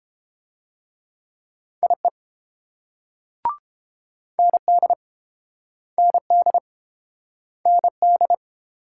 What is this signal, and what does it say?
Morse code, heard as a keyed tone, about 31 wpm, 700 Hz. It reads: IE ND ND ND